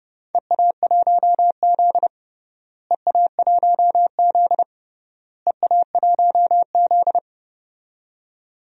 Morse code, heard as a keyed tone, about 30 wpm, 700 Hz. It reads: EA1Z EA1Z EA1Z
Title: DXCC-100 Call Signs - 4 Characters - Repeated 3x - 30wpm